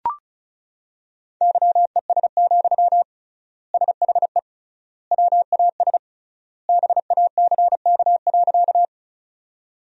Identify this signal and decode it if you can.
Morse code, heard as a keyed tone, about 34 wpm, 700 Hz. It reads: YES, SHE WAS BACK.
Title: 3x Word-Spacing - Sentences from Top 500 Words - 35wpm